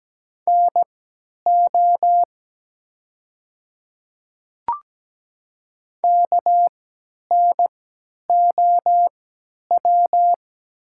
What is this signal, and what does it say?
Morse code, heard as a keyed tone, about 17 wpm, spaced as for 10 wpm, 700 Hz. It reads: NO KNOW
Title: 3x Character Spacing - Top 100 Words - 17wpm